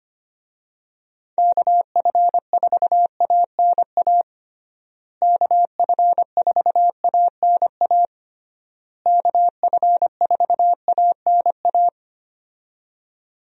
Morse code, heard as a keyed tone, about 25 wpm, 700 Hz. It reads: KF4ANA KF4ANA KF4ANA
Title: USA Call Signs - 6 Characters - Repeated 3x - 25wpm